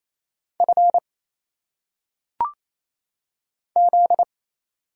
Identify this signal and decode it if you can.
Morse code, heard as a keyed tone, about 27 wpm, 700 Hz. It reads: F Z
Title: Single Letters - Phonetic Pronunciation 28wpm